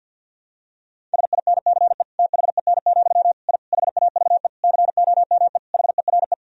Morse code, heard as a keyed tone, about 51 wpm, 700 Hz. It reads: SINCE THEN, I HAVE COME HERE
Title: Sentences from Top 1000 Words - Review Part 1 - 50wpm